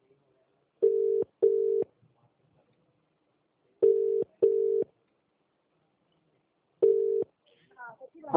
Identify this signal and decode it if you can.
Ringback tone (UK), 400 + 450 Hz, with speech.